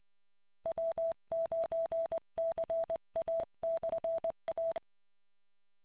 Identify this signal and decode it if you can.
Morse code, heard as a keyed tone, about 20 wpm, 660 Hz. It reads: W9CA/A